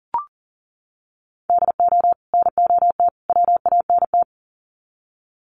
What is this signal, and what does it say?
Morse code, heard as a keyed tone, about 40 wpm, 700 Hz. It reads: DO NOT WANT
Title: Sets of 3 Words 40wpm